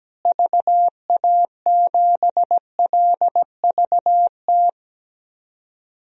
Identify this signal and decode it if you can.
Morse code, heard as a keyed tone, about 17 wpm, 700 Hz. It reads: VA7LVT
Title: DXCC-100 Call Signs - 6 Characters - Repeated 3x - 17wpm